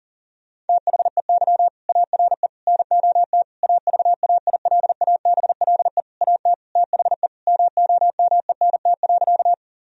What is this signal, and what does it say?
Morse code, heard as a keyed tone, about 40 wpm, 700 Hz. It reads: THEY ARE NOT AVAILABLE AT THE MOMENT.
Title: Sentences from Top 1000 Words - Review Part 3 - 40wpm